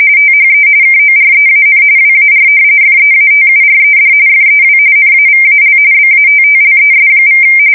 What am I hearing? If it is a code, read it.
RTTY, about 45.45 baud. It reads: THE KOREAN WAR FEATURED FEW NAVAL BATTLES=
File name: fsk170 / usnavy5